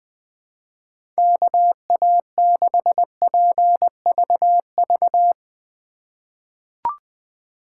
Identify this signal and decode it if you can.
Morse code, heard as a keyed tone, about 20 wpm, 700 Hz. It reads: KA6PVV
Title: USA Call Signs - 6 Characters - 2x Character Spacing - 20wpm